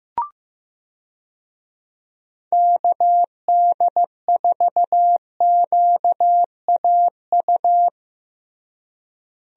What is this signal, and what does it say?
Morse code, heard as a keyed tone, about 15 wpm, 700 Hz. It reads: KD4QAU
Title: USA Call Signs - 6 Characters - Repeated 3x - 15wpm